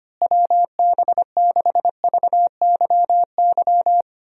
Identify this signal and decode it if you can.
Morse code, heard as a keyed tone, about 25 wpm, 700 Hz. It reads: WB6VYY
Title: USA Call Signs - 6 Characters - Repeated 2x - 25wpm